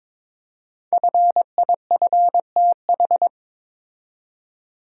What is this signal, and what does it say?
Morse code, heard as a keyed tone, about 23 wpm, 690 Hz. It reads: FIFTH
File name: 5 Letter Words - First 3 Letters Given - 22wpm